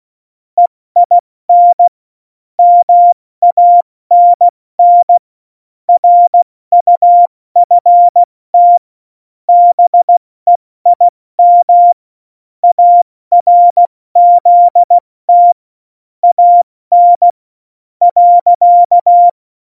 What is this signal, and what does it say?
Morse code, heard as a keyed tone, about 16 wpm, spaced as for 13 wpm, 700 Hz. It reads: EIN MANN RUFT BEIM ARZT AN.
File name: Morsekurs / Witze / Witz 16. Januar 2018, 70 F80